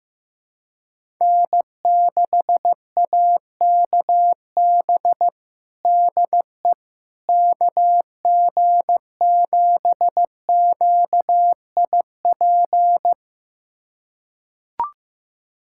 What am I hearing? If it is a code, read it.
Morse code, heard as a keyed tone, about 15 wpm, 700 Hz. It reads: N6AKB DE KG7QIP
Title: USA Callsign DE Callsign 15wpm